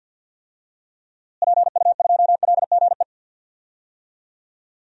Morse code, heard as a keyed tone, about 52 wpm, 700 Hz. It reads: PU2LGE
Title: DXCC-100 Call Signs - 6 Characters - Repeated 2x - 50wpm